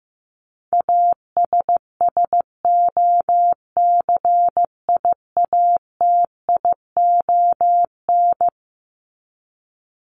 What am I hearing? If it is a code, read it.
Morse code, heard as a keyed tone, about 15 wpm, 700 Hz. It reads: ASSOCIATION
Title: Top 1001-1250 Words 15wpm